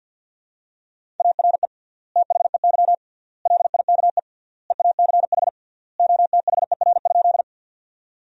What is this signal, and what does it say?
Morse code, heard as a keyed tone, about 50 wpm, 700 Hz. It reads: ARE THEY LIKE EACH OTHER?
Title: 3x Word-Spacing - Sentences from Top 200 Words - 50wpm